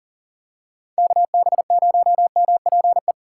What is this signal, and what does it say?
Morse code, heard as a keyed tone, about 38 wpm, 700 Hz. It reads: KB0MPE
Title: USA Call Signs - 6 Characters - 6x Character Spacing - 40wpm